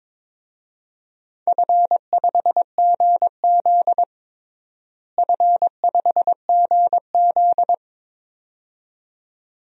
Morse code, heard as a keyed tone, about 23 wpm, 700 Hz. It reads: F5GZ F5GZ
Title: DXCC-100 Call Signs - 4 Characters - Repeated 2x - 22wpm